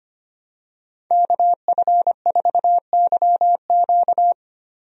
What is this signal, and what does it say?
Morse code, heard as a keyed tone, about 25 wpm, 700 Hz. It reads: KF4YQ
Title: USA Call Signs - 5 Characters - Repeated 2x - 25wpm